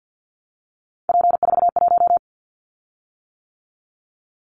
Morse code, heard as a keyed tone, about 50 wpm, 700 Hz. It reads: P41
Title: Callsign Prefixes for Top 100 DXCC - 50wpm